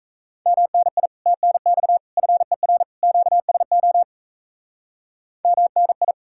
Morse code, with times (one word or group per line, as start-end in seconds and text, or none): 0.5-1.1 MNI
1.3-2.0 TNX
2.2-2.8 FER
3.0-4.0 QSO
5.5-6.1 MNI